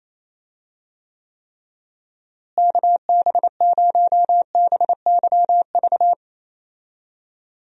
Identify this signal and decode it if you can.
Morse code, heard as a keyed tone, about 28 wpm, 700 Hz. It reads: KB0BYV
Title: USA Call Signs - 6 Characters - Repeated 3x - 28wpm